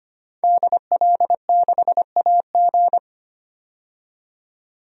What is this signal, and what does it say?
Morse code, heard as a keyed tone, about 25 wpm, 700 Hz. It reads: DL6AG
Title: DXCC-100 Call Signs - 5 Characters - 25wpm